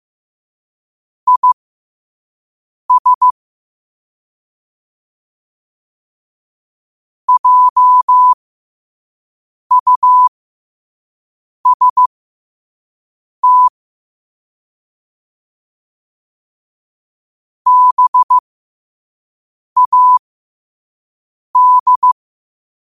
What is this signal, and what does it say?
Morse code, heard as a keyed tone, about 15 wpm, spaced as for 5 wpm, 1000 Hz. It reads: IS JUST BAD
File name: qotdfiles / 5wpm